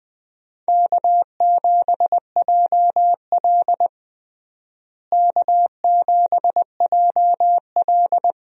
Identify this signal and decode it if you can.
Morse code, heard as a keyed tone, about 20 wpm, 700 Hz. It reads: K7JL K7JL
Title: USA Call Signs - 4 Characters - Repeated 2x - 20wpm